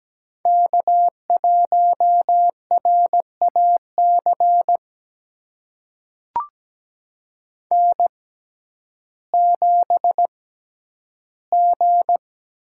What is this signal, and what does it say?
Morse code, heard as a keyed tone, about 17 wpm, spaced as for 10 wpm, 700 Hz. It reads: K1RAC N7G
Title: USA Call Signs - 5 Characters - 6x Character Spacing - 17wpm